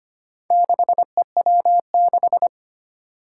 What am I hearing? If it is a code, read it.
Morse code, heard as a keyed tone, about 25 wpm, 700 Hz. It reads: 6EW6
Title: Four Challenging Letter-Number Combinations 25wpm